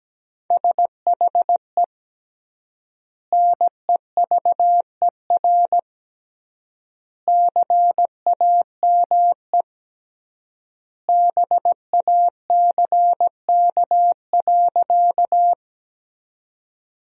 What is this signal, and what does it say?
Morse code, heard as a keyed tone, about 17 wpm, 700 Hz. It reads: SHE NEVER CAME BACK.
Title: 3x Word-Spacing - Sentences from Top 300 Words - 17wpm